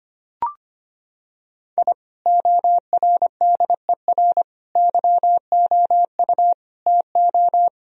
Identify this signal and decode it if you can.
Morse code, heard as a keyed tone, about 25 wpm, 700 Hz. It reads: I ORDER YOU TO
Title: Sentences from Top 1000 Words - Review Part 2 - 25wpm